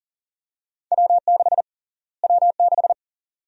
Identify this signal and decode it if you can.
Morse code, heard as a keyed tone, about 40 wpm, 700 Hz. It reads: W6 W6
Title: USA Call Sign Prefixes - Repeated 2x - 40wpm